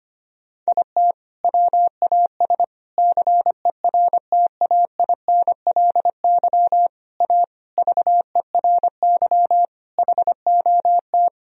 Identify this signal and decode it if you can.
Morse code, heard as a keyed tone, about 25 wpm, 700 Hz. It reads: IT WAS CERTAINLY A VERY HOT
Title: Sentences from Top 1000 Words - Review Part 3 - 25wpm